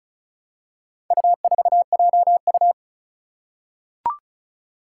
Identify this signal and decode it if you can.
Morse code, heard as a keyed tone, about 36 wpm, 700 Hz. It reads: U4JU
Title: Four Challenging Letter-Number Combinations 35wpm